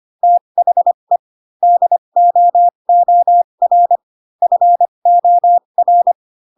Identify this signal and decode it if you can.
Morse code, heard as a keyed tone, about 25 wpm, spaced as for 22 wpm, 700 Hz. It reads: THE DOOR FOR
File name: Pinocchio / Pinocchio ch7 18 25 ik1hge